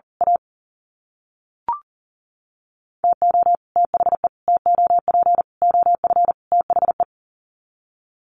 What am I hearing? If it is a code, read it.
Morse code, heard as a keyed tone, about 40 wpm, 700 Hz. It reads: A TO THE TOP OF THE
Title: Sets of 5 Words 40wpm